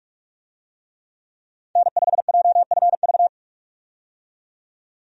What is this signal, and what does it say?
Morse code, heard as a keyed tone, about 45 wpm, 700 Hz. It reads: N5JFV
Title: USA Call Signs - 5 Characters - 45wpm